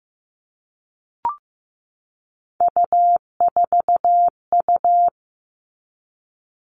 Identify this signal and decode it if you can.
Morse code, heard as a keyed tone, about 15 wpm, 700 Hz. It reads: U4U
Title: All Cut Numbers - 3-Digit 15wpm